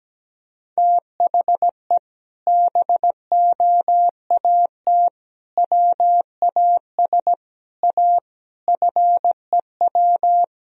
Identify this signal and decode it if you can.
Morse code, heard as a keyed tone, about 17 wpm, 700 Hz. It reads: THE BOAT WAS A FEW
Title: Sentences from Top 1000 Words - Review Part 3 - 17wpm